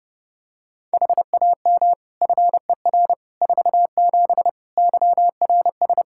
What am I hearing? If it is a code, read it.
Morse code, heard as a keyed tone, about 31 wpm, 700 Hz. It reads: HAM FER 47 YRS
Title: K1USN SST QSO Practice 30wpm